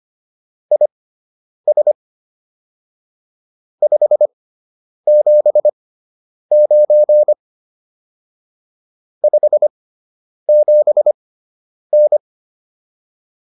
Morse code, heard as a keyed tone, about 25 wpm, spaced as for 9 wpm, 600 Hz. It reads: IS 579 57N